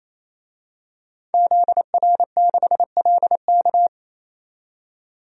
Morse code, heard as a keyed tone, about 28 wpm, 700 Hz. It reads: ZR6LK
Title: DXCC-100 Call Signs - 5 Characters - 28wpm